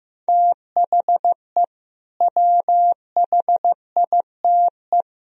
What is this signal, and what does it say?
Morse code, heard as a keyed tone, about 15 wpm, 700 Hz. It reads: THE WHITE
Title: Sentences from Top 1000 Words - Review Part 1 - 15wpm